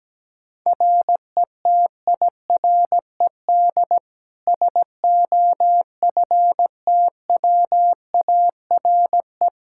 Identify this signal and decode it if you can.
Morse code, heard as a keyed tone, about 17 wpm, 700 Hz. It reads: RETIRED SOFTWARE
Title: Practice QSO - Exchange 3 - More Info - 17wpm